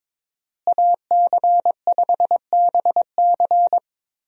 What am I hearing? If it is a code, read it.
Morse code, heard as a keyed tone, about 22 wpm, 700 Hz. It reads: AC5BC
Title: USA Call Signs - 5 Characters - Repeated 2x - 22wpm